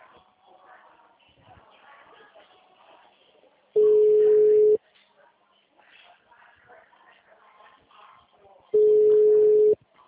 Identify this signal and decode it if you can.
Ringback tone (Europe, 425 Hz).
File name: out- -2014- - - .216